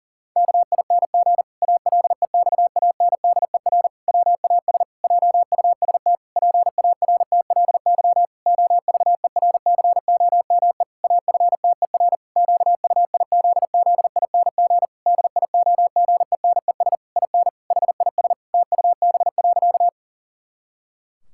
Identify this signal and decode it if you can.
Morse code, heard as a keyed tone, about 40 wpm, 700 Hz. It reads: KING ALEXANDER WAS JUST PARTLY OVERCOME AFTER QUIZZING DIOGENES IN HIS TUB.
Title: Pangram Challenge v2 - 40wpm